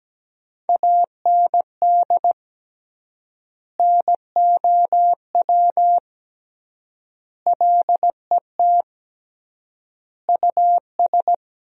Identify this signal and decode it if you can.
Morse code, heard as a keyed tone, about 17 wpm, 700 Hz. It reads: AND NOW LET US
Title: 3x Word-Spacing - Sentences from Top 300 Words - 17wpm